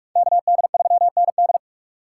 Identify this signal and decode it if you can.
Morse code, heard as a keyed tone, about 44 wpm, 700 Hz. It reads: KD3ND